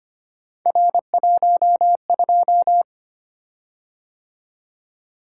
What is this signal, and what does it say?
Morse code, heard as a keyed tone, about 25 wpm, 700 Hz. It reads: R12
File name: Three Letter-Number Combinations - 25wpm